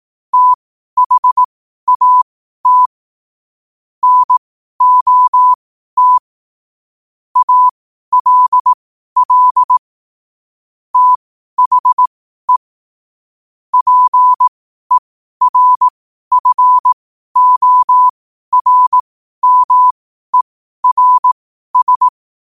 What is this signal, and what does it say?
Morse code, heard as a keyed tone, about 18 wpm, spaced as for 12 wpm, 1000 Hz. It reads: THAT NOT ALL THE PERFORMERS